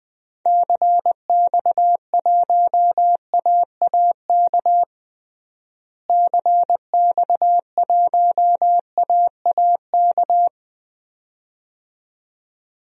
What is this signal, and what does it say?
Morse code, heard as a keyed tone, about 20 wpm, 700 Hz. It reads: CX1AAK CX1AAK
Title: DXCC-100 Call Signs - 6 Characters - Repeated 2x - 20wpm